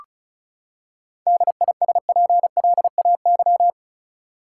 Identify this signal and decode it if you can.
Morse code, heard as a keyed tone, about 34 wpm, 700 Hz. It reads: DISPLAY
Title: Seven Letter Words - 35wpm